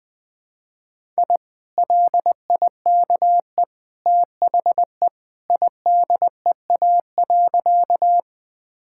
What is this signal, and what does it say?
Morse code, heard as a keyed tone, about 20 wpm, 700 Hz. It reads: I LIKE THE IDEA.
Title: Sentences with 2-Word Phrases - Spoken Beginning - 20wpm